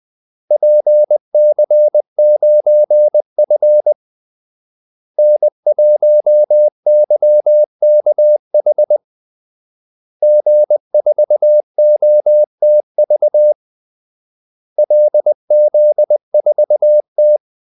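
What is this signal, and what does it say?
Morse code, heard as a keyed tone, about 20 wpm, 600 Hz. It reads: PC9F N1YKH G4OTV LZ4T
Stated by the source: Morse call-sign practice at 20 wpm.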